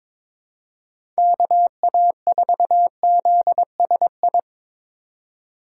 Morse code, heard as a keyed tone, about 22 wpm, 700 Hz. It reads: KA4ZSI
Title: USA Call Signs - 6 Characters - 2x Character Spacing - 22wpm